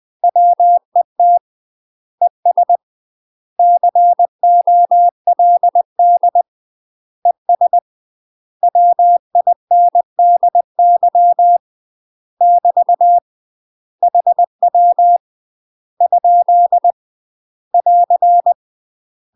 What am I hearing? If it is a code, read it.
Morse code, heard as a keyed tone, about 20 wpm, 700 Hz. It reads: WET ES COLD ES WINDY <BT> HW ? <AR>
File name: UK MASTER QSOs TRAINING / 20WPM / UK MASTER QSOs 0010